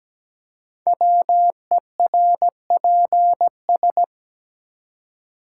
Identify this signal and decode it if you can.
Morse code, heard as a keyed tone, about 17 wpm, 700 Hz. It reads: WERPS
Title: Code Groups - 5 Letters - 17wpm